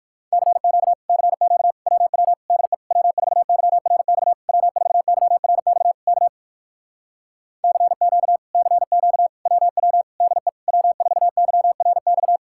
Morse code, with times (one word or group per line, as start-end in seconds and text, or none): 0.3-0.9 CQ
1.1-1.7 CQ
1.9-2.3 WW
2.5-2.7 DE
2.9-4.3 W4YRX
4.5-5.9 W4YRX
6.1-6.3 K
7.6-8.4 CQ
8.5-9.3 CQ
9.5-10.0 WW
10.2-10.5 DE
10.7-12.4 W4YRX